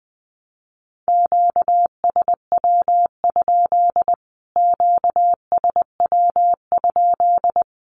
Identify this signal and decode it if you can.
Morse code, heard as a keyed tone, about 20 wpm, 700 Hz. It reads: QSW? QSW?
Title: Q-Codes Repeated 3x 20wpm